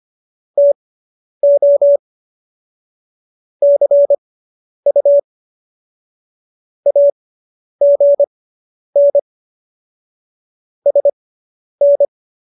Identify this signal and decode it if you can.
Morse code, heard as a keyed tone, about 25 wpm, spaced as for 10 wpm, 570 Hz. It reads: TO CU AGN SN